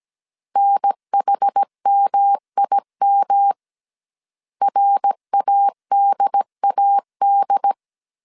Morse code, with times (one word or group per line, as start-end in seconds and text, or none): 0.6-3.5 NHMIM
4.6-7.7 RADAD